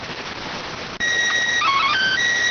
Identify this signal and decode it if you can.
SSTV (slow-scan television), PD 290.